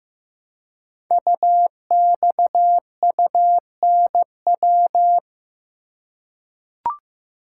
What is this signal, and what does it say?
Morse code, heard as a keyed tone, about 15 wpm, 700 Hz. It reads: UXUNW
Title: Code Groups - 5 Letters - 15wpm